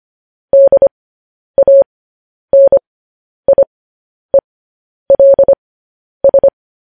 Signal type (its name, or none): Morse code